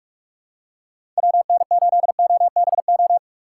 Morse code, heard as a keyed tone, about 46 wpm, 700 Hz. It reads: WN8OBO
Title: USA Call Signs - 6 Characters - 4x Character Spacing - 45wpm